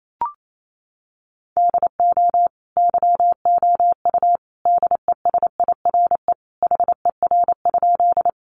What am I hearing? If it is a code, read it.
Morse code, heard as a keyed tone, about 28 wpm, 700 Hz. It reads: DO YOU DESIRE HER?
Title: Sentences from Top 1100 Words 28wpm